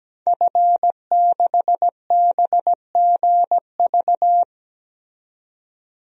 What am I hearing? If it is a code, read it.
Morse code, heard as a keyed tone, about 17 wpm, 700 Hz. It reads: F6BGV
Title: DXCC-100 Call Signs - 5 Characters - 17wpm